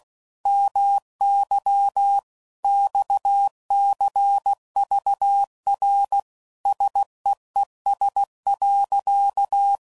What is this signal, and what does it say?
Morse code, heard as a keyed tone, about 16 wpm, 800 Hz. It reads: MY XCVR SEES.
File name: CW Code Groups @15WPM / QSOs / WD0FBT Spaced